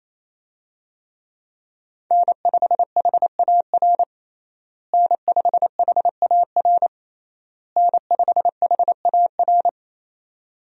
Morse code, heard as a keyed tone, about 27 wpm, 700 Hz. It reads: N5HAR N5HAR N5HAR
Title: USA Call Signs - 5 Characters - Repeated 3x - 28wpm